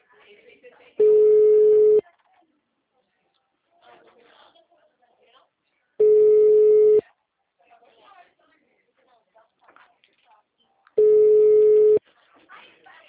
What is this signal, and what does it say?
Ringback tone (Europe, 425 Hz), with speech.
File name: out- -2009- - - .1002